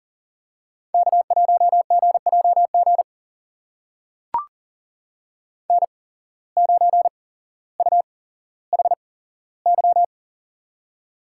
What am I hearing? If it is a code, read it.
Morse code, heard as a keyed tone, about 39 wpm, spaced as for 17 wpm, 700 Hz. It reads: K1GJG N9UHY
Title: USA Call Signs - 5 Characters - 8x Character Spacing - 40wpm